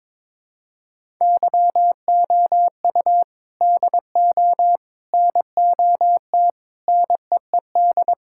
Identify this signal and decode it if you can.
Morse code, heard as a keyed tone, about 22 wpm, 700 Hz. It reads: YOU DO NOT NEED
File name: practice / 22wpm / Sentences with 3-Word Phrases - 22wpm